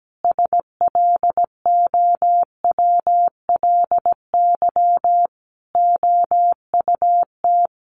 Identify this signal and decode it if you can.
Morse code, heard as a keyed tone, about 17 wpm, 700 Hz. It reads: SLOWLY OUT
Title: A Sound Of Thunder 17wpm